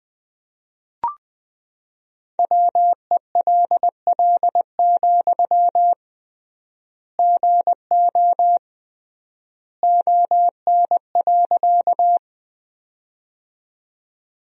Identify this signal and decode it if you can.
Morse code, heard as a keyed tone, about 20 wpm, 700 Hz. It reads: WELL, GO ON.
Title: 3x Word-Spacing - Sentences from Top 200 Words - 20wpm